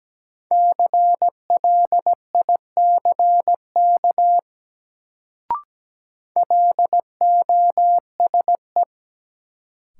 Morse code, heard as a keyed tone, about 17 wpm, 700 Hz. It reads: CLICK LOSE
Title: Sentences from Top 1100 Words - New Word Drill - 17wpm